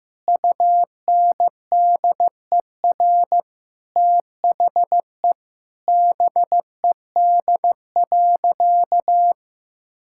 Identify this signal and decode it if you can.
Morse code, heard as a keyed tone, about 16 wpm, 700 Hz. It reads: UNDER THE BED.
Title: Sentences from Top 1000 Words - Review Part 5 - 15wpm